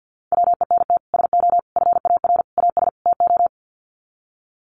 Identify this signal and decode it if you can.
Morse code, heard as a keyed tone, about 50 wpm, 700 Hz. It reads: WENT SO FAR AS TO
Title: Sets of 5 Words Encore 50wpm